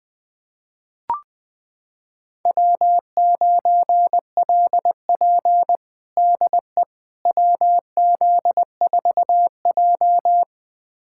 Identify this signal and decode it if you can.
Morse code, heard as a keyed tone, about 20 wpm, 700 Hz. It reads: W9LP DE WZ4J
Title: USA Callsign DE Callsign - 4 Characters - 20wpm